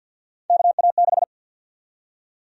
Morse code, heard as a keyed tone, about 52 wpm, 700 Hz. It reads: KA6
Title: USA Call Sign Prefixes - 50wpm